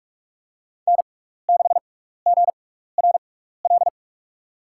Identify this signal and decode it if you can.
Morse code, heard as a keyed tone, about 44 wpm, spaced as for 15 wpm, 690 Hz. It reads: N6GRL
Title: USA Call Signs - 5 Characters - 6x Character Spacing - 45wpm